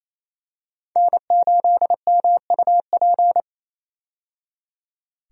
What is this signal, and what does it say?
Morse code, heard as a keyed tone, about 28 wpm, 700 Hz. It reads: N8MUP
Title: USA Call Signs - 5 Characters - 28wpm